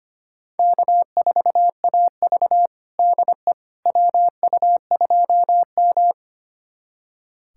Morse code, heard as a keyed tone, about 25 wpm, 700 Hz. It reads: K4AV DE WU2M
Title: USA Callsign DE Callsign - 4 Characters - 25wpm